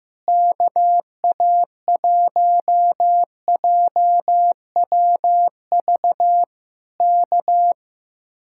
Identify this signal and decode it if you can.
Morse code, heard as a keyed tone, about 15 wpm, 700 Hz. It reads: KA1JWV K
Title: Speed-Racing Common QSO Phrases Encore - 15wpm